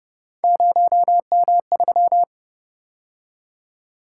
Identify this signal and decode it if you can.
Morse code, heard as a keyed tone, about 31 wpm, 700 Hz. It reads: 0M3